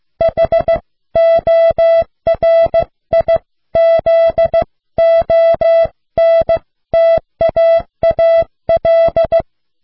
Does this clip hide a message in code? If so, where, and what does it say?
Yes: Morse code from 0.2 to 9.4 s, heard as a keyed tone, about 15 wpm, 660 Hz. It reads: HORIZONTAAL